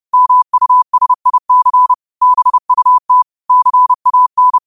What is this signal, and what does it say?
Morse code, heard as a keyed tone, about 30 wpm, 1000 Hz. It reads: MUSIC BUT CAN
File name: qotdfiles / 30wpm